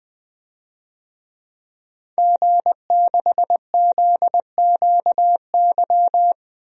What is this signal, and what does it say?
Morse code, heard as a keyed tone, about 20 wpm, 700 Hz. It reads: G6ZQY
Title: DXCC-100 Call Signs - 5 Characters - 20wpm